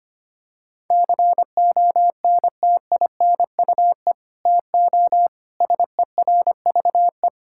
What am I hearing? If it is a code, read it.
Morse code, heard as a keyed tone, about 25 wpm, 700 Hz. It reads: CONTINUE TO SERVE
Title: Sentences from Top 1000 Words - Review Part 6 - 25wpm